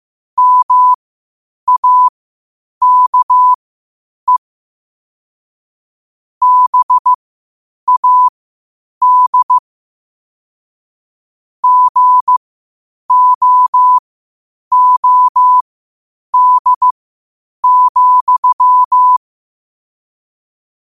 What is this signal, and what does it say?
Morse code, heard as a keyed tone, about 15 wpm, spaced as for 8 wpm, 1000 Hz. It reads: MAKE BAD GOOD,
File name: qotdfiles / 8wpm